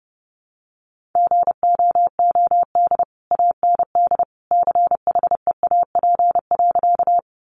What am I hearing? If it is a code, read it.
Morse code, heard as a keyed tone, about 30 wpm, 700 Hz. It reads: GOOD AND CHEAP.
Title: Sentences from Top 1200 Words 30wpm